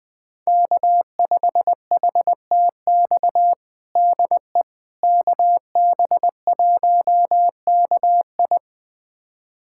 Morse code, heard as a keyed tone, about 20 wpm, 700 Hz. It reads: K5HTX DE KB1KI
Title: USA Callsign DE Callsign - 5 Characters - 20wpm